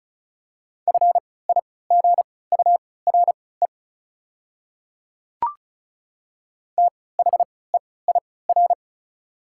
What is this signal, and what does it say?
Morse code, heard as a keyed tone, about 34 wpm, spaced as for 20 wpm, 700 Hz. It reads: FIGURE THEIR
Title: 3x Character Spacing - Top 500 Words - 35wpm